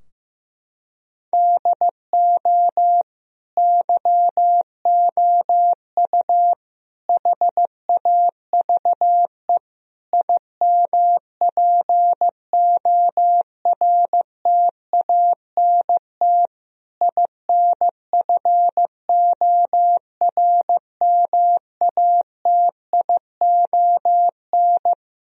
Morse code, heard as a keyed tone, about 16 wpm, 700 Hz. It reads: DO YOU HAVE IMPORTANT INFORMATION
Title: Sentences from Top 1000 Words - Review Part 3 - 15wpm